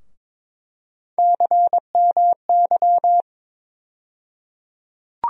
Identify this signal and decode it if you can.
Morse code, heard as a keyed tone, about 22 wpm, 700 Hz. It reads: CMY
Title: USA Call Sign Postfixes - 3 Letters 22wpm